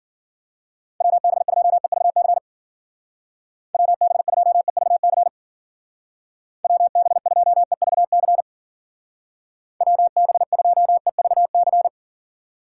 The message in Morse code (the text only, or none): WB2EVC WB2EVC WB2EVC WB2EVC